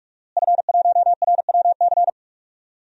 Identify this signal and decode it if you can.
Morse code, heard as a keyed tone, about 44 wpm, 700 Hz. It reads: F1RWC
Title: DXCC-100 Call Signs - 5 Characters - Repeated 3x - 45wpm